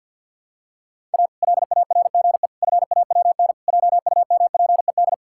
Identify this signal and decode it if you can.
Morse code, heard as a keyed tone, about 51 wpm, 700 Hz. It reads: A LARGE FAWN JUMPED
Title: Pangram Challenge v2 - 50wpm